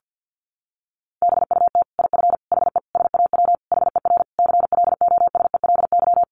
Morse code, heard as a keyed tone, about 50 wpm, 700 Hz. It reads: BUT IF HE SAW HER CLOSELY
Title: Sentences from Top 1400 Words 50wpm